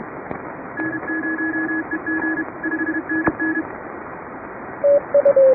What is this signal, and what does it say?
Morse code, heard as a keyed tone, about 32 wpm, 1700 Hz, with radio noise. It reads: N0EG 5NN
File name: chatham / 2350-12CW